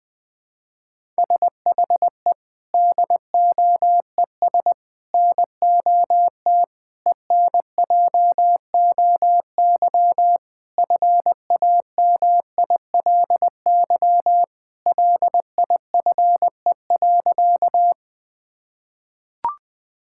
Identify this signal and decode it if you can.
Morse code, heard as a keyed tone, about 19 wpm, 700 Hz. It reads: SHE DOES NOT ENJOY FAMILY LIFE.
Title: Sentences from Top 1000 Words - Review Part 5 - 20wpm